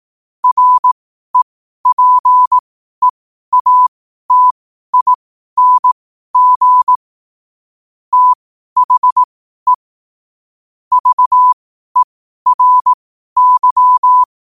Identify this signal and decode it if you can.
Morse code, heard as a keyed tone, about 18 wpm, spaced as for 12 wpm, 1000 Hz. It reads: REPEATING THE VERY
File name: qotdfiles / 12wpm